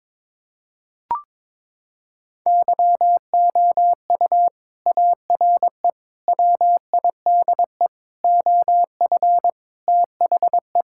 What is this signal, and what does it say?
Morse code, heard as a keyed tone, about 22 wpm, 700 Hz. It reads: YOU ARE WIDE OF THE
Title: Sentences from Top 1000 Words - Review Part 6 - 22wpm